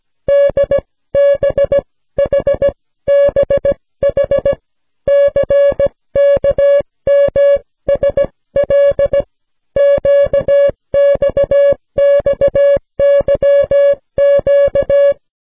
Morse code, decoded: DBHBH CKMSL QXXYQ